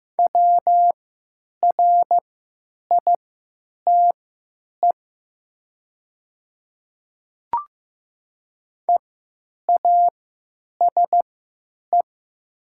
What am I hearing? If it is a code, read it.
Morse code, heard as a keyed tone, about 15 wpm, spaced as for 9 wpm, 700 Hz. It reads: WRITE EASE
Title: 3x Character Spacing - Top 500 Words - 15wpm